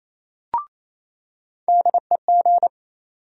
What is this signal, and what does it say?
Morse code, heard as a keyed tone, about 27 wpm, 700 Hz. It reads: DEG
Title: USA Call Sign Postfixes - 3 Letters 28wpm